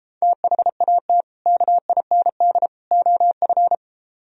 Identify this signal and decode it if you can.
Morse code, heard as a keyed tone, about 33 wpm, 700 Hz. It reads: THAT KIND OF